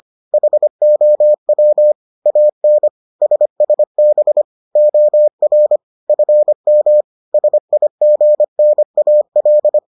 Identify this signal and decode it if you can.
Morse code, heard as a keyed tone, about 25 wpm, 600 Hz. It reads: HOW AN SSB OR FM SIGNAL